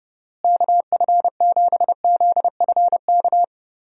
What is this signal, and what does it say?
Morse code, heard as a keyed tone, about 29 wpm, 700 Hz. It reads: KF7ZFK
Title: USA Call Signs - 6 Characters - Repeated 3x - 30wpm